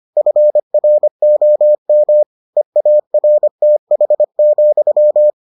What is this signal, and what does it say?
Morse code, heard as a keyed tone, about 25 wpm, 600 Hz. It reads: FROM EARTH,